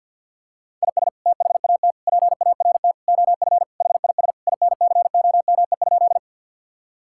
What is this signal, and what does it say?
Morse code, heard as a keyed tone, about 50 wpm, 700 Hz. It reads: IS THAT PART OF HIS INCOME?
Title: Sentences from Top 1000 Words - Review Part 5 - 50wpm